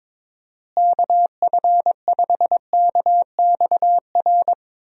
Morse code, heard as a keyed tone, about 22 wpm, 700 Hz. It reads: KF5KXR
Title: USA Call Signs - 6 Characters - Repeated 3x - 22wpm